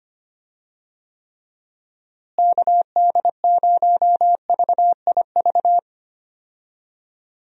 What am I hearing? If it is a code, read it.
Morse code, heard as a keyed tone, about 25 wpm, 700 Hz. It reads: KD0VIV
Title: USA Call Signs - 6 Characters - Repeated 3x - 25wpm